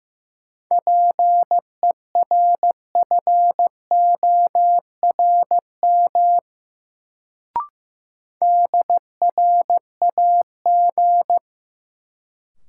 Morse code, heard as a keyed tone, about 15 wpm, 700 Hz. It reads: PERFORM DRAG
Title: Sentences from Top 1500 Words - New Word Drill Alt - 15wpm